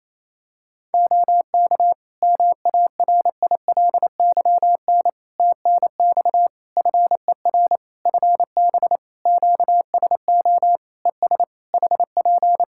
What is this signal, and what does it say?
Morse code, heard as a keyed tone, about 27 wpm, 700 Hz. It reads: OK MARILYN TNX FER FB QSO ES HP
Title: Practice QSO - Exchange 4 - Ending - 28wpm